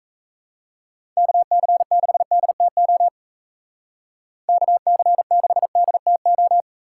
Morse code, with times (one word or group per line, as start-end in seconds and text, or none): 1.2-3.1 KC6DTO
4.5-6.6 KC6DTO